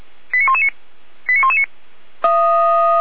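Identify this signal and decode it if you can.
Five-tone selective call (ZVEI), heard as quick runs of five tones, tones of about 70 ms. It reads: 89119 89119
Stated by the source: ZVEI five-tone selective call received on an SDR.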